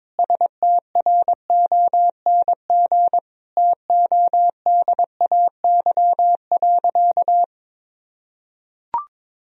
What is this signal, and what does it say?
Morse code, heard as a keyed tone, about 22 wpm, 700 Hz. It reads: STRONG TODAY.
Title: Sentences from Top 1000 Words - Review Part 2 - 22wpm